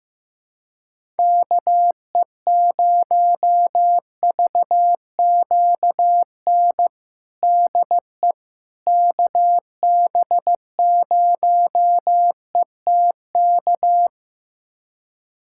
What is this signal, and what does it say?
Morse code, heard as a keyed tone, about 15 wpm, 700 Hz. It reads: KE0VQN DE KB0ETK
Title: USA Callsign DE Callsign 15wpm